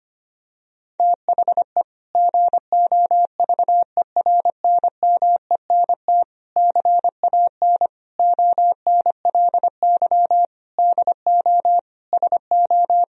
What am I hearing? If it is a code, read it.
Morse code, heard as a keyed tone, about 25 wpm, 700 Hz. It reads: THE GOVERNMENT CAN ONLY DO SO
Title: Sentences from Top 1000 Words - Review Part 3 - 25wpm